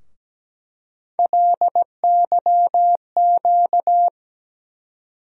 Morse code, heard as a keyed tone, about 17 wpm, 700 Hz. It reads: LYQ
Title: USA Call Sign Postfixes - 3 Letters 17wpm